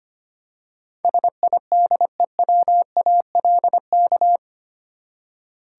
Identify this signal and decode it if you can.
Morse code, heard as a keyed tone, about 25 wpm, 700 Hz. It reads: SIDEWALK
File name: Eight Letter Words - 25wpm